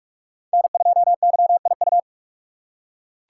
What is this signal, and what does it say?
Morse code, heard as a keyed tone, about 44 wpm, 700 Hz. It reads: N2YIU